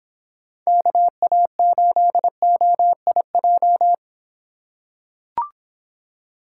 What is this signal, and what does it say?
Morse code, heard as a keyed tone, about 26 wpm, 700 Hz. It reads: KA8OIJ